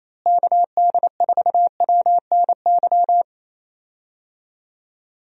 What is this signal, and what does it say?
Morse code, heard as a keyed tone, about 28 wpm, 700 Hz. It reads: KD4WNY
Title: USA Call Signs - 6 Characters - Repeated 2x - 28wpm